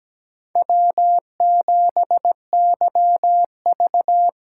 Morse code, heard as a keyed tone, about 17 wpm, 700 Hz. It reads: W7YV
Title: USA Call Signs - 4 Characters - 2x Character Spacing - 17wpm